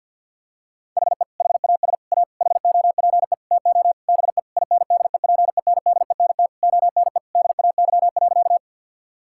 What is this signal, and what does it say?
Morse code, heard as a keyed tone, about 50 wpm, 700 Hz. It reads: HE HAS A HOPE TO BE INDEPENDENT ONE DAY.
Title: Sentences from Top 1000 Words - Review Part 6 - 50wpm